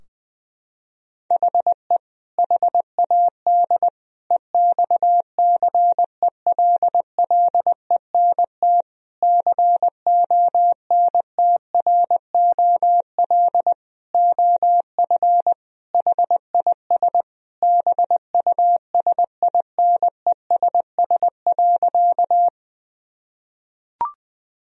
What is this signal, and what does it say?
Morse code, heard as a keyed tone, about 21 wpm, 700 Hz. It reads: HE HAD EXCELLENT CONTROL OF HIS BUSINESS.
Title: Sentences from Top 1000 Words - Review Part 3 - 20wpm